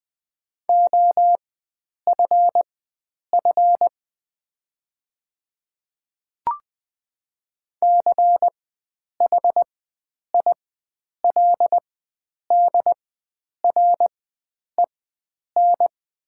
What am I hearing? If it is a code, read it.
Morse code, heard as a keyed tone, about 20 wpm, spaced as for 9 wpm, 700 Hz. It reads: OFF CHILDREN
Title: 4x Character Spacing - Top 300 Words - 20wpm